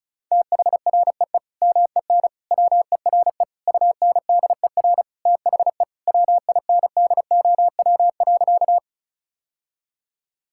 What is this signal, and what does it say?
Morse code, heard as a keyed tone, about 34 wpm, 700 Hz. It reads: THREE MEN WERE UNDER THE WINDOW.
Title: Sentences from Top 1600 Words 35wpm